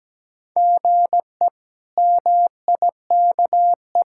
Morse code, heard as a keyed tone, about 17 wpm, 700 Hz. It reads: GE MIKE
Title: K1USN SST QSO Practice - 17wpm